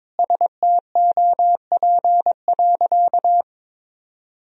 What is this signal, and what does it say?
Morse code, heard as a keyed tone, about 22 wpm, 700 Hz. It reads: STOP.
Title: Sentences from Top 1000 Words - Review Part 3 - 22wpm